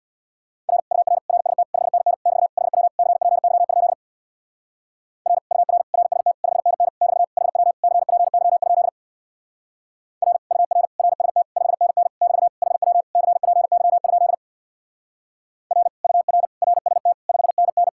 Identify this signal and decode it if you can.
Morse code, heard as a keyed tone, about 62 wpm, 700 Hz. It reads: R UR RST 5NN <BT> HW CPY? R UR RST 5NN <BT> HW CPY? R UR RST 5NN <BT> HW CPY? R UR RST 5NN
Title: Speed-Racing Common QSO Phrases - 50wpm